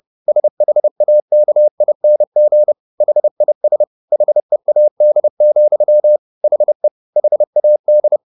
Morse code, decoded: SHAKING HIS HEAD, HE HAD

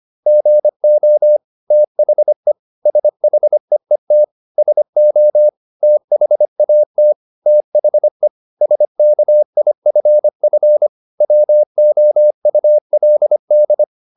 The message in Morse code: GO THE SHEET SO THAT THE SKIFF WOULD

